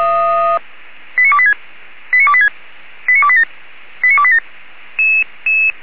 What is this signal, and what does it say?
Five-tone selective call (ZVEI), heard as quick runs of five tones, tones of about 70 ms. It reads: 89286 89286 89287 89287